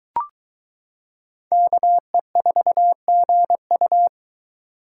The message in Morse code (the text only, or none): KE4GU